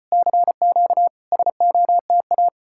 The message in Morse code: CQ SOTA

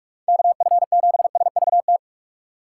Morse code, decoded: KF7SVT

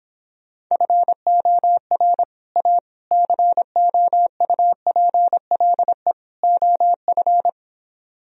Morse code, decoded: FOR A COUPLE OF

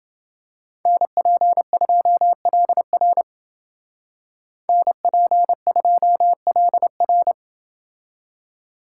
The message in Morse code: NP2LR NP2LR

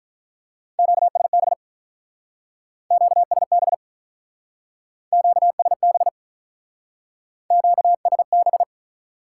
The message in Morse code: QSB QSB QSB QSB

